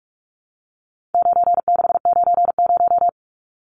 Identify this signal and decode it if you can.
Morse code, heard as a keyed tone, about 44 wpm, 700 Hz. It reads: 9690
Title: Numbers - 4-Digit 45wpm